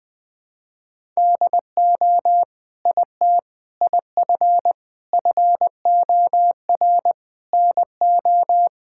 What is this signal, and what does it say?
Morse code, heard as a keyed tone, about 20 wpm, 700 Hz. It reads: DO IT IF FOR NO
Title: Sentences with 7-Word Phrases - 20wpm